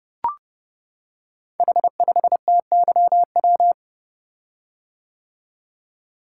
Morse code, heard as a keyed tone, about 30 wpm, 700 Hz. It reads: H5TYW
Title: Code Groups - 5 Letter:Numbers - 30wpm